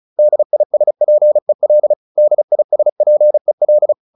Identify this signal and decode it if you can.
Morse code, heard as a keyed tone, about 34 wpm, 600 Hz. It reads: DISPEL DISPEL
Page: advanced-practice-files